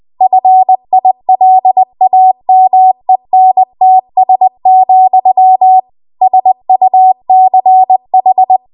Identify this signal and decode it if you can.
Morse code, heard as a keyed tone, about 20 wpm, 750 Hz. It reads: FILAMENTS, SUCH
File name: Archive / 20 WPM / 20WPM